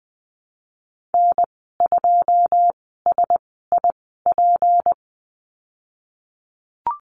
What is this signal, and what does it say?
Morse code, heard as a keyed tone, about 20 wpm, spaced as for 15 wpm, 700 Hz. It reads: N2SIP
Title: 154 Lesson 25g - Letter p - Call Signs 20wpm